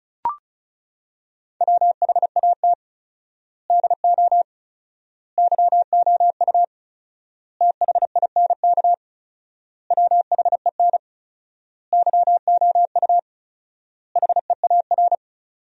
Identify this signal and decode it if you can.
Morse code, heard as a keyed tone, about 36 wpm, 700 Hz. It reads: WHAT DO YOU THINK WHEN YOU HEAR
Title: 4x Word-Spacing - Sentences from Top 300 Words - 35wpm